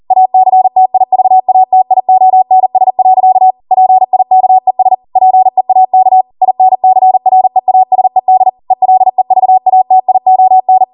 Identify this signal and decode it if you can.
Morse code, heard as a keyed tone, about 40 wpm, 750 Hz. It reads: ACTIVATIONS. PIKES PEAK INCREASED ELEVATION